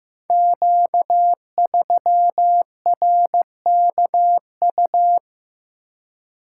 Morse code, heard as a keyed tone, about 15 wpm, 700 Hz. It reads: Q3RKU
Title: Code Groups - 5 Letter:Numbers - 15wpm